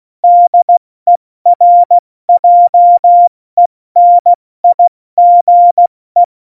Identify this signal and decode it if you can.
Morse code, heard as a keyed tone, about 16 wpm, 700 Hz. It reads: DERJENIGE